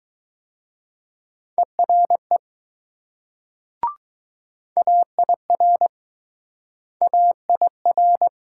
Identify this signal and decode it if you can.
Morse code, heard as a keyed tone, about 20 wpm, 700 Hz. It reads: ERE AIR AIR